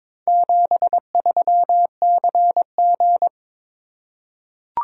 Morse code, heard as a keyed tone, about 22 wpm, 700 Hz. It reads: 73CG
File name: practice / 22wpm / Four Letter-Number Combinations - 22wpm